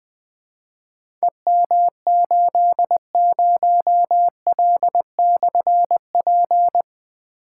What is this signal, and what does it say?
Morse code, heard as a keyed tone, about 21 wpm, 700 Hz. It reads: EM80L/P
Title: Difficult Call Signs 20wpm